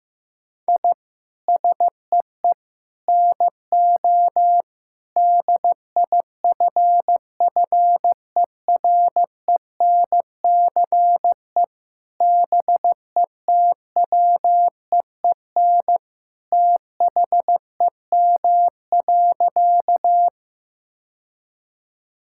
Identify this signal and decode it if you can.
Morse code, heard as a keyed tone, about 15 wpm, 700 Hz. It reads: I SEE NO DIFFERENCE BETWEEN THEM.
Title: Sentences from Top 1000 Words - Review Part 5 - 15wpm